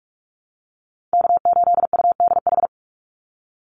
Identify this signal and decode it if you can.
Morse code, heard as a keyed tone, about 44 wpm, 700 Hz. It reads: K8UDH
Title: Call Signs - 45wpm